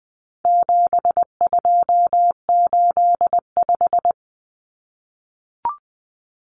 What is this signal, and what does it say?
Morse code, heard as a keyed tone, about 20 wpm, 700 Hz. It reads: 7285
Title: Common Cut Numbers - 4-Digit 20wpm